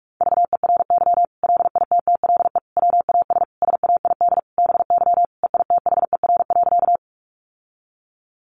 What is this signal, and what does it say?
Morse code, heard as a keyed tone, about 44 wpm, 700 Hz. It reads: VERY LITTLE WAS SAID BY EITHER.
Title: Sentences from Top 600 Words 45wpm